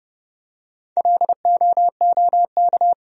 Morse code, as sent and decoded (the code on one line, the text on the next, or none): .-.. --- --- -.-
LOOK